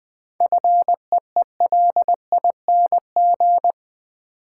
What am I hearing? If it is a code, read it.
Morse code, heard as a keyed tone, about 21 wpm, 700 Hz. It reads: FEELING
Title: Sentences from Top 1200 Words - New Word Drill Alt - 20wpm